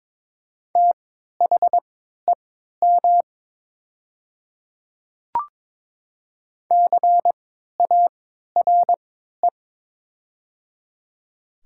Morse code, heard as a keyed tone, about 22 wpm, spaced as for 12 wpm, 700 Hz. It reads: THEM CARE
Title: 3x Character Spacing - Top 300 Words - 22wpm